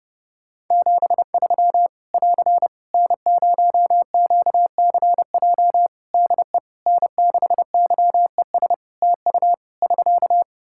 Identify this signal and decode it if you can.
Morse code, heard as a keyed tone, about 30 wpm, 700 Hz. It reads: 73 <AR> N0QCJ DE N6YES TU <SK>
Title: K1USN SST QSO Practice 30wpm